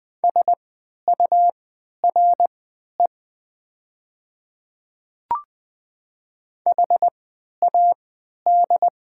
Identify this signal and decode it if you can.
Morse code, heard as a keyed tone, about 20 wpm, spaced as for 11 wpm, 700 Hz. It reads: SURE HAD
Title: 3x Character Spacing - Top 500 Words - 20wpm